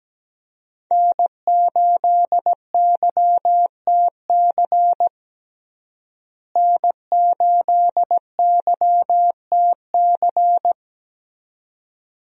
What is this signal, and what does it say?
Morse code, heard as a keyed tone, about 17 wpm, 700 Hz. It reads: N8YTC N8YTC
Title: USA Call Signs - 5 Characters - Repeated 2x - 17wpm